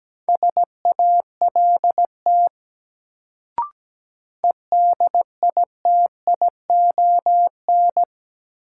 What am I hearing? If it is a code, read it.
Morse code, heard as a keyed tone, about 17 wpm, 700 Hz. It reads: SALT EDITION
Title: Sets of 5 Words - New Word Drill - 17wpm